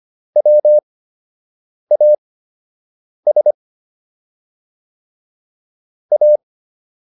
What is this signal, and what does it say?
Morse code, heard as a keyed tone, about 25 wpm, spaced as for 7 wpm, 600 Hz. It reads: WAS A